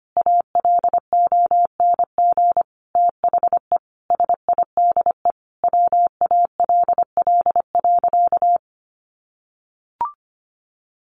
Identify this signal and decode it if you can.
Morse code, heard as a keyed tone, about 25 wpm, 700 Hz. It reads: ALONG THE SIDE WALL.
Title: Sentences from Top 1100 Words 25wpm